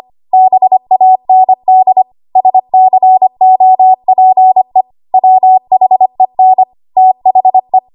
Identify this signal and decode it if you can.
Morse code, heard as a keyed tone, about 25 wpm, 750 Hz. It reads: BAND SCOPE WHEN THE
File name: Archive / 25 WPM / 25WPM